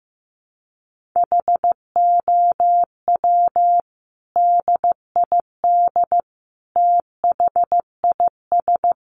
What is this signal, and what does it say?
Morse code, heard as a keyed tone, about 15 wpm, 700 Hz. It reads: HOW DID THIS
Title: Sentences from Top 1400 Words 15wpm